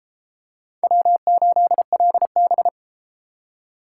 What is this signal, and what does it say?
Morse code, heard as a keyed tone, about 29 wpm, 700 Hz. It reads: W8LB